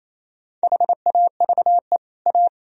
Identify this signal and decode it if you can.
Morse code, heard as a keyed tone, about 34 wpm, 700 Hz. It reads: HAVE A
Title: Speed-Racing - Sets of 2 Words 28wpm